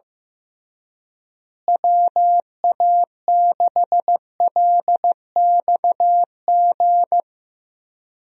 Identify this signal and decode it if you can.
Morse code, heard as a keyed tone, about 16 wpm, 700 Hz. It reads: WA6LXG
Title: USA Call Signs - 6 Characters - Repeated 3x - 15wpm